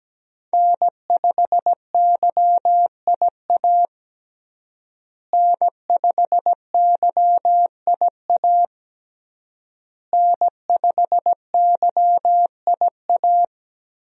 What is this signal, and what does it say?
Morse code, heard as a keyed tone, about 17 wpm, 700 Hz. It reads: N5YIA N5YIA N5YIA
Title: USA Call Signs - 5 Characters - Repeated 3x - 17wpm